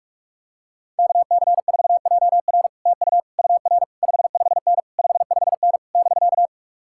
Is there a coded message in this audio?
Yes: Morse code from 1.0 to 6.5 s, heard as a keyed tone, about 49 wpm, 700 Hz. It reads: KC4JR TU UR 55N 55N <BK>